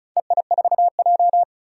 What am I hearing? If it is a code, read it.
Morse code, heard as a keyed tone, about 34 wpm, 700 Hz. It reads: EI4J
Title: DXCC-100 Call Signs - 4 Characters - Repeated 3x - 35wpm